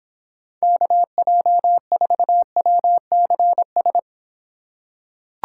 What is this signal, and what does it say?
Morse code, heard as a keyed tone, about 24 wpm, 700 Hz. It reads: KJ4WCS